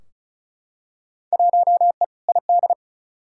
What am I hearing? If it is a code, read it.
Morse code, heard as a keyed tone, about 36 wpm, 700 Hz. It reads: 1E ID
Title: Field Day Exchanges - 35wpm